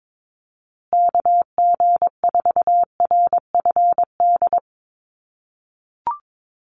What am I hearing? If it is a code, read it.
Morse code, heard as a keyed tone, about 22 wpm, 700 Hz. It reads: KG4RFD
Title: USA Call Signs Repeated 3x 22wpm